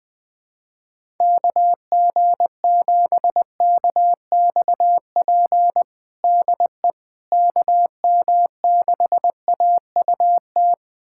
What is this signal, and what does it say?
Morse code, heard as a keyed tone, about 21 wpm, 700 Hz. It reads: KG7KXP DE KM6AUT
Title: USA Callsign DE Callsign - 6 Characters - 20wpm